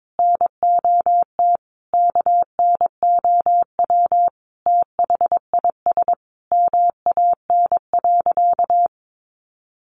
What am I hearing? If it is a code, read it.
Morse code, heard as a keyed tone, about 22 wpm, 700 Hz. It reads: NOT KNOW THIS MAN.
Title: Sentences from Top 600 Words 22wpm